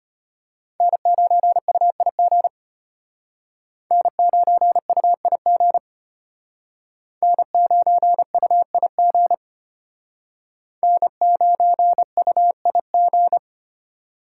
Morse code, heard as a keyed tone, about 32 wpm, 700 Hz. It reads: N9UIG N9UIG N9UIG N9UIG